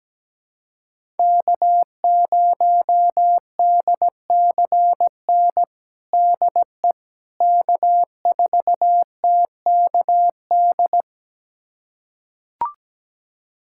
Morse code, heard as a keyed tone, about 17 wpm, 700 Hz. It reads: K0DCN DE K4TKD
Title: USA Callsign DE Callsign - 5 Characters - 17wpm